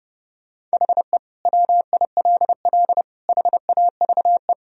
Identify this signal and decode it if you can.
Morse code, heard as a keyed tone, about 30 wpm, 700 Hz. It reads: HE WILL HAVE
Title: Sentences from Top 1000 Words - Review Part 4 - 30wpm